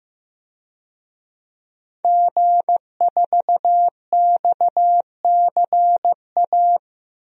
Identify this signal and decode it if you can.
Morse code, heard as a keyed tone, about 15 wpm, 700 Hz. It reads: G4XCA
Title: DXCC-100 Call Signs - 5 Characters - 15wpm